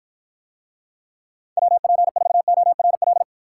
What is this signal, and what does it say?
Morse code, heard as a keyed tone, about 53 wpm, 700 Hz. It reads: WP4ORL